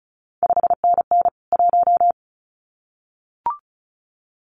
Morse code, heard as a keyed tone, about 36 wpm, 700 Hz. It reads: 5NN 1
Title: CQ WW DX Contest Exchange Encore - 35wpm